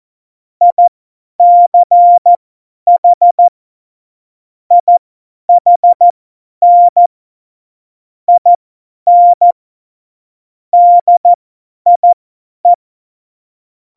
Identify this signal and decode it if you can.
Morse code, heard as a keyed tone, about 14 wpm, spaced as for 10 wpm, 700 Hz. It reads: ICH IHN IN DIE
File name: Morsekurs / Witze / Witz 26. Dezember 2017, 50 F70